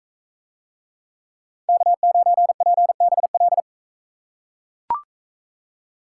Morse code, heard as a keyed tone, about 42 wpm, 700 Hz. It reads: K9PBL